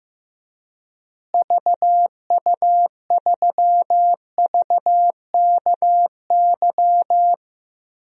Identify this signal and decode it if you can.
Morse code, heard as a keyed tone, about 15 wpm, 700 Hz. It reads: VU3VKY
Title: DXCC-100 Call Signs - 6 Characters - Repeated 3x - 15wpm